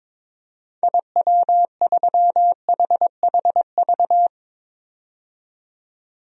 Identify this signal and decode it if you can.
Morse code, heard as a keyed tone, about 22 wpm, 700 Hz. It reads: IW3HHV
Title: DXCC-100 Call Signs - 6 Characters - 22wpm